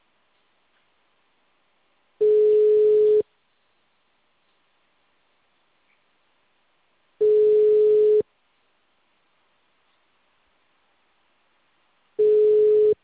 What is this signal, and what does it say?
Ringback tone (Europe, 425 Hz).